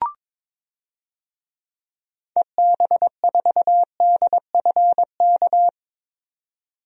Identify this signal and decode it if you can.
Morse code, heard as a keyed tone, about 22 wpm, 700 Hz. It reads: EB4DFK
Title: DXCC-100 Call Signs - 6 Characters - 22wpm